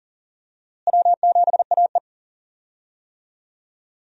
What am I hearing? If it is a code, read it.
Morse code, heard as a keyed tone, about 40 wpm, 700 Hz. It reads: W7AE
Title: USA Call Signs - 4 Characters - Repeated 2x - 40wpm